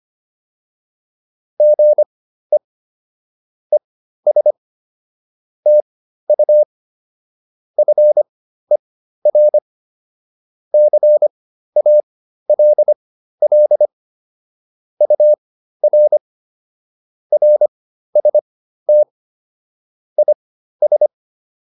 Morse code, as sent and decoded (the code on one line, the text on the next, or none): --. / . ... / - ..- / ..-. .-. / -.-. .- .-.. .-.. / ..- .-. / .-. ... - / .. ...
GE ES TU FER CALL UR RST IS